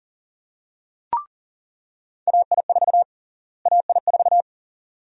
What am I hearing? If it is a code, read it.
Morse code, heard as a keyed tone, about 40 wpm, 700 Hz. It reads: AI4 AI4
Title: USA Call Sign Prefixes - Repeated 2x - 40wpm